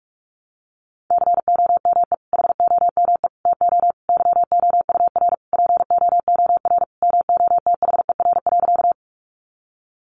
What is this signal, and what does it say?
Morse code, heard as a keyed tone, about 44 wpm, 700 Hz. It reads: COME HOME TO YOUR POOR MOTHER.
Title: Sentences from Top 1000 Words 45wpm